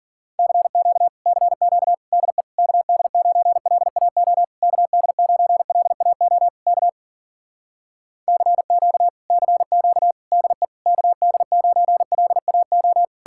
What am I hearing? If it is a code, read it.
Morse code, heard as a keyed tone, about 53 wpm, 700 Hz. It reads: CQ CQ DE KD9LAO KD9LAO K CQ CQ DE KD9LAO